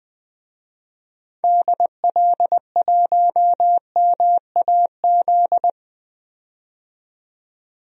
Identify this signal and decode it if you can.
Morse code, heard as a keyed tone, about 20 wpm, 700 Hz. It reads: DL1MAZ
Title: DXCC-100 Call Signs - 6 Characters - 20wpm